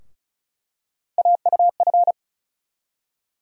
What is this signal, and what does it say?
Morse code, heard as a keyed tone, about 36 wpm, 700 Hz. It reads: AUF
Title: USA Call Sign Postfixes - 3 Letters 35wpm